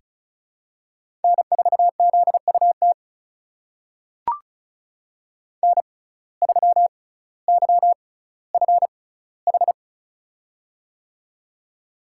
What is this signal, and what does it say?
Morse code, heard as a keyed tone, about 34 wpm, spaced as for 18 wpm, 700 Hz. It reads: N4ZUT N3YFH
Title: USA Call Signs - 5 Characters - 6x Character Spacing - 35wpm